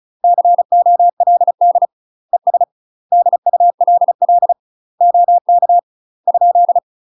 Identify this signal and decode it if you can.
Morse code, heard as a keyed tone, about 34 wpm, 700 Hz. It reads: COLD ES DULL OK ?
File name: UK MASTER QSOs TRAINING / 35WPM / UK MASTER QSOs 0034